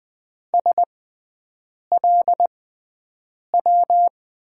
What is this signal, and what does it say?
Morse code, heard as a keyed tone, about 20 wpm, spaced as for 7 wpm, 700 Hz. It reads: SLW
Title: USA Call Signs - 6 Characters - 6x Character Spacing - 20wpm